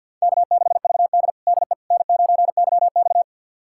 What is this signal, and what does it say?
Morse code, heard as a keyed tone, about 50 wpm, 700 Hz. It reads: K6VD DE N9YX
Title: USA Callsign DE Callsign - 4 Characters - 50wpm